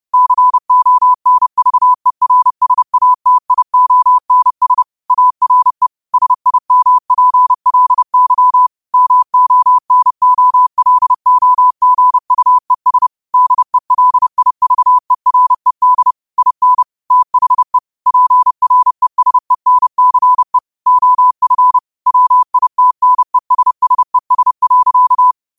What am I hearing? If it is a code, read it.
Morse code, heard as a keyed tone, about 30 wpm, 1000 Hz. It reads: CONVERSATIONS ARE SIMPLY MONOLOGUES DELIVERED IN THE PRESENCE OF WITNESSES.
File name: qotdfiles / 30wpm